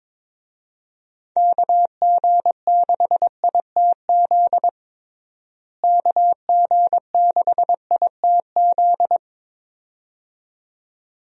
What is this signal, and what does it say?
Morse code, heard as a keyed tone, about 22 wpm, 700 Hz. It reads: KG6ITZ KG6ITZ
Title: USA Call Signs - 6 Characters - Repeated 2x - 22wpm